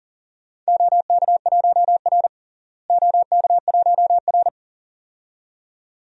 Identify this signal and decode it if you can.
Morse code, heard as a keyed tone, about 40 wpm, 700 Hz. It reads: OK1R OK1R
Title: DXCC-100 Call Signs - 4 Characters - Repeated 2x - 40wpm